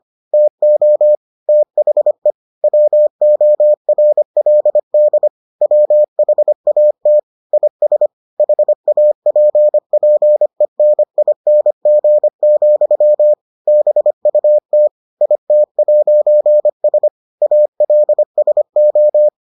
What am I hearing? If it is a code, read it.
Morse code, heard as a keyed tone, about 25 wpm, 600 Hz. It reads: TO THE WORLD WHAT IS HAPPENING, BUT IT'S ALSO